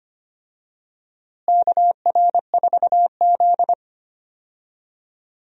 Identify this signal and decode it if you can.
Morse code, heard as a keyed tone, about 25 wpm, 700 Hz. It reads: KR4Z
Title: USA Call Signs - 4 Characters - Repeated 3x - 25wpm